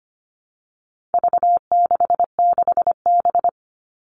Morse code, heard as a keyed tone, about 25 wpm, 700 Hz. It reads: V66B
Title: All Cut Numbers - 4-Digit 25wpm